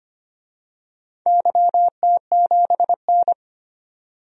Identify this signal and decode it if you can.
Morse code, heard as a keyed tone, about 25 wpm, 700 Hz. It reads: YT7N
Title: DXCC-100 Call Signs - 4 Characters - 25wpm